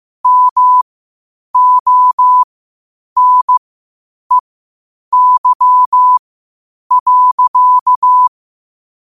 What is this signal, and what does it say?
Morse code, heard as a keyed tone, about 15 wpm, spaced as for 8 wpm, 1000 Hz. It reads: MONEY.
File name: qotdfiles / 8wpm